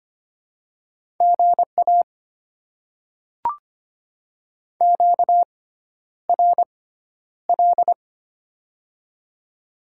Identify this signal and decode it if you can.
Morse code, heard as a keyed tone, about 25 wpm, spaced as for 11 wpm, 700 Hz. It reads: GA QRL
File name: Top 20 QSO Elements - 6x Character Spacing - 25wpm